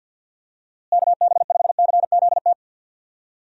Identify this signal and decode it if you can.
Morse code, heard as a keyed tone, about 52 wpm, 700 Hz. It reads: KB5CZT
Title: USA Call Signs - 6 Characters - Repeated 3x - 50wpm